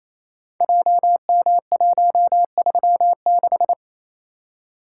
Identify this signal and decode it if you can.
Morse code, heard as a keyed tone, about 29 wpm, 700 Hz. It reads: JM136
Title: Code Groups - 5 Letter:Numbers - 28wpm